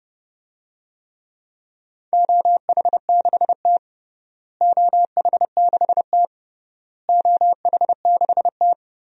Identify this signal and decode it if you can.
Morse code, heard as a keyed tone, about 30 wpm, 700 Hz. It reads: OH6T OH6T OH6T
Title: DXCC-100 Call Signs - 4 Characters - Repeated 3x - 30wpm